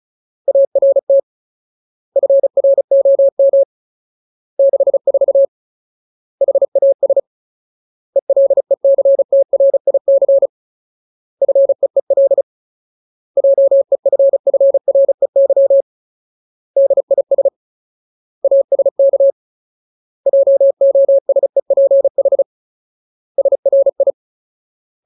Morse code, heard as a keyed tone, about 34 wpm, 550 Hz. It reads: ART FROM 64 HAS ELECTRIC FEEL JEFFREY DIS ASK JOSEPH SRI
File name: Ultimate Word List / 35WPM Ultimate